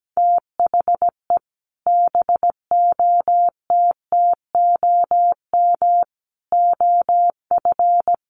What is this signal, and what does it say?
Morse code, heard as a keyed tone, about 17 wpm, 700 Hz. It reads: THE BOTTOM OF
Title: Sets of 3 Words 17wpm